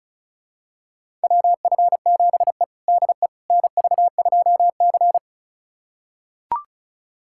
Morse code, heard as a keyed tone, about 34 wpm, 700 Hz. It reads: WF7E DE NV2C
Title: USA Callsign DE Callsign - 4 Characters - 35wpm